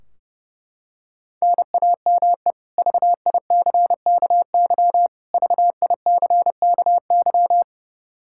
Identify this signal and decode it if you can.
Morse code, heard as a keyed tone, about 30 wpm, 700 Hz. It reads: NAME VICKY VICKY
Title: K1USN SST QSO Practice 30wpm